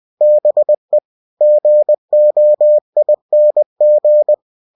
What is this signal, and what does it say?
Morse code, heard as a keyed tone, about 20 wpm, 600 Hz. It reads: BE GOING
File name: cw-books / winnie-the-pooh / winnie-20wpm-page-0016